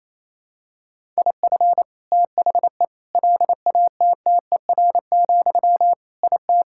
Morse code, heard as a keyed tone, about 28 wpm, 700 Hz. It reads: IF THE LATTER, IT
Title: Sentences from Top 1600 Words 28wpm